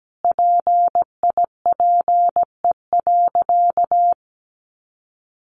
Morse code, heard as a keyed tone, about 17 wpm, 700 Hz. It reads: PIPE.
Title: Sentences from Top 1500 Words 17wpm